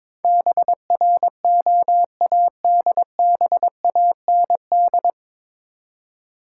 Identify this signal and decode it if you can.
Morse code, heard as a keyed tone, about 22 wpm, 700 Hz. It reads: BROADBAND
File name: practice / 22wpm / Nine Letter Words - 22wpm